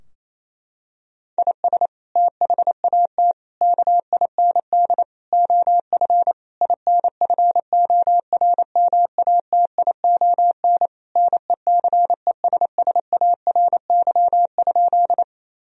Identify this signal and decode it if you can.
Morse code, heard as a keyed tone, about 28 wpm, 700 Hz. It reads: IS THAT KIND OF INFORMATION NECESSARY?
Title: Sentences from Top 1000 Words - Review Part 3 - 28wpm